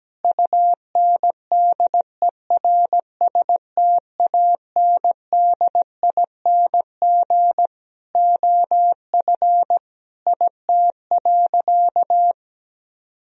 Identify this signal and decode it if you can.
Morse code, heard as a keyed tone, about 17 wpm, 700 Hz. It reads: UNDERSTANDING OF IT.
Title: Sentences from Top 1000 Words - Review Part 3 - 17wpm